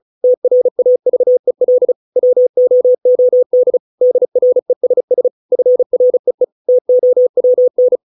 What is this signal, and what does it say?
Morse code, heard as a keyed tone, about 34 wpm, 500 Hz. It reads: TRAVEL WOOD DRESS FREE TOWN